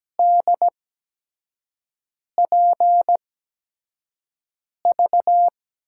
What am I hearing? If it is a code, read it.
Morse code, heard as a keyed tone, about 17 wpm, spaced as for 5 wpm, 700 Hz. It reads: DPV
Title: USA Call Signs - 6 Characters - 8x Character Spacing - 17wpm